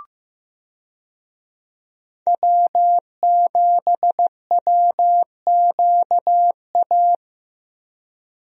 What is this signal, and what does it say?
Morse code, heard as a keyed tone, about 15 wpm, 700 Hz. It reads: W7WQA
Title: USA Call Signs - 5 Characters - Repeated 3x - 15wpm